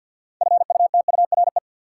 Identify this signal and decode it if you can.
Morse code, heard as a keyed tone, about 52 wpm, 700 Hz. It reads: FUTURE
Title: Sets of 3 Words - New Word Drill - 50wpm